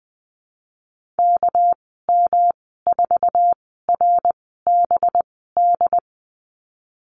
Morse code, heard as a keyed tone, about 20 wpm, spaced as for 15 wpm, 700 Hz. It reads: KM4RBD
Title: 219 Lesson 35g - Number 0 - Call Signs 20wpm